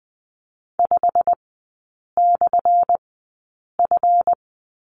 Morse code, heard as a keyed tone, about 20 wpm, spaced as for 8 wpm, 700 Hz. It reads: 5/F
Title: 191 Lesson 31e - Letter v - Three Characters 20wpm